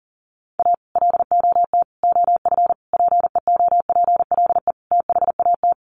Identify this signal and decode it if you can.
Morse code, heard as a keyed tone, about 40 wpm, 700 Hz. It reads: A LOT OF PEOPLE THAT
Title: Sets of 5 Words Encore2 40wpm